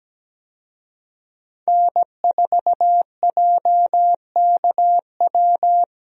Morse code, heard as a keyed tone, about 17 wpm, 700 Hz. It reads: N4JKW
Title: USA Call Signs - 5 Characters - Repeated 3x - 17wpm